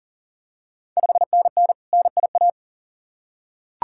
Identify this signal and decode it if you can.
Morse code, heard as a keyed tone, about 40 wpm, 690 Hz. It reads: 5NN NIA
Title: Ontario QSO Party Exchange 40wpm